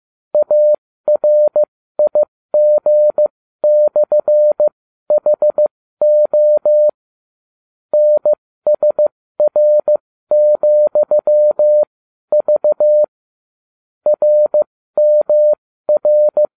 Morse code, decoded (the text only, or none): ARIG/HO NSR,V RMR